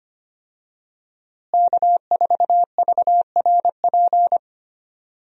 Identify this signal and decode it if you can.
Morse code, heard as a keyed tone, about 25 wpm, 700 Hz. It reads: K4VRP
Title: USA Call Signs - 5 Characters - Repeated 2x - 25wpm